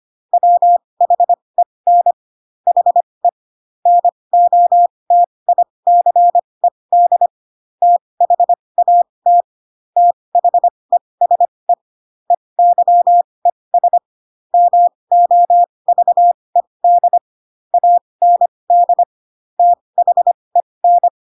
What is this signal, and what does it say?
Morse code, heard as a keyed tone, about 25 wpm, spaced as for 20 wpm, 700 Hz. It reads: WHEN HE NOTICED THAT THESE EYES MOVED AND THEN
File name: Pinocchio / Pinocchio ch3 15 25 ik1hge